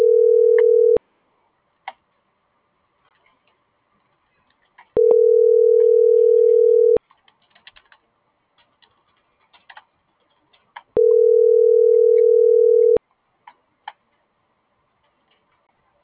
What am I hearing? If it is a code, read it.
Ringback tone (North America), 440 + 480 Hz.
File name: CENTRAL GENERAL / out karla.sky - .20384